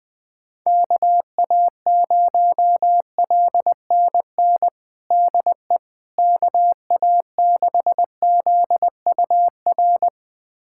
Morse code, heard as a keyed tone, about 20 wpm, 700 Hz. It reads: KA0LNN DE KA6ZUR
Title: USA Callsign DE Callsign - 6 Characters - 20wpm